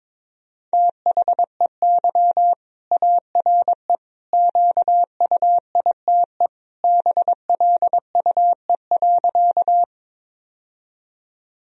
Morse code, heard as a keyed tone, about 21 wpm, 700 Hz. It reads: THEY ARE QUITE BLUE.
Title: Sentences from Top 1000 Words - Review Part 3 - 22wpm